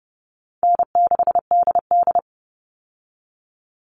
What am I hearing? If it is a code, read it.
Morse code, heard as a keyed tone, about 30 wpm, 700 Hz. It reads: N6DD
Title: All Cut Numbers - 4-Digit 30wpm